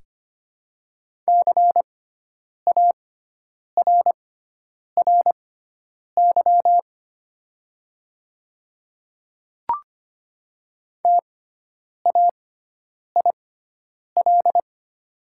Morse code, heard as a keyed tone, about 25 wpm, spaced as for 9 wpm, 690 Hz. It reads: CARRY TAIL
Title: 6x Character Spacing - Top 500 Words - 25wpm